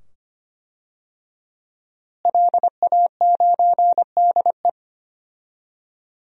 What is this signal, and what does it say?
Morse code, heard as a keyed tone, about 25 wpm, 700 Hz. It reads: LA9DE
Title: DXCC-100 Call Signs - 5 Characters - Repeated 2x - 25wpm